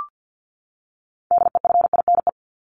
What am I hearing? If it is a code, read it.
Morse code, heard as a keyed tone, about 50 wpm, 700 Hz. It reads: DEFINE
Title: ED Suffix - Top 100 Words with Base Words 50wpm